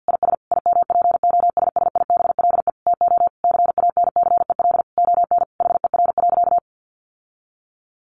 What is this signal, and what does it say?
Morse code, heard as a keyed tone, about 49 wpm, 700 Hz. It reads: IS IMPOSSIBLE TO CANCEL ON HER.
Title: Sentences from Top 1200 Words 50wpm